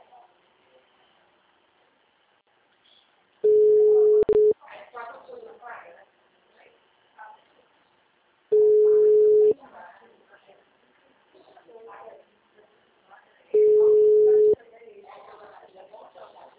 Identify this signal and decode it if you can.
Ringback tone (Europe, 425 Hz), with speech.